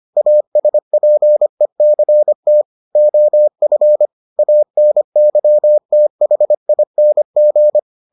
Morse code, heard as a keyed tone, about 25 wpm, 600 Hz. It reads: ASPECT OF ANYTHING